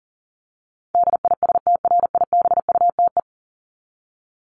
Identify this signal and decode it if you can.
Morse code, heard as a keyed tone, about 40 wpm, 700 Hz. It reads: DISTRIBUTE
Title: Top 1501-1750 Words 40wpm